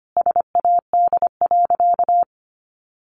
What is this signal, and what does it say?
Morse code, heard as a keyed tone, about 25 wpm, 700 Hz. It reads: SAD.
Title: Additional Easy Phrases 25wpm